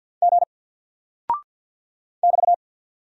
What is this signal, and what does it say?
Morse code, heard as a keyed tone, about 48 wpm, 700 Hz. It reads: G <BT>
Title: Single Character (Letter:Number:Punctuation:Prosign) 50wpm